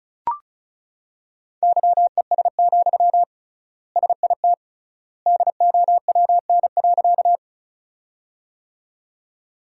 Morse code, heard as a keyed tone, about 36 wpm, 700 Hz. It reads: YES, SIT DOWN.
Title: 3x Word-Spacing - Sentences from Top 500 Words - 35wpm